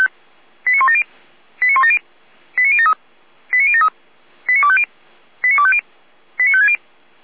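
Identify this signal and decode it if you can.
Five-tone selective call (ZVEI), heard as quick runs of five tones, tones of about 70 ms. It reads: 89170 89170 89073 89073 89250 89250 89560
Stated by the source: Five-tone ZVEI selective call received on an SDR receiver.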